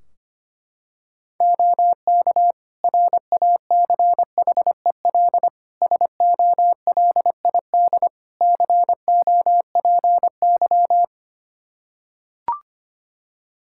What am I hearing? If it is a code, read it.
Morse code, heard as a keyed tone, about 25 wpm, 700 Hz. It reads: OK RACHEL SOLID COPY
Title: K1USN SST QSO Practice 25wpm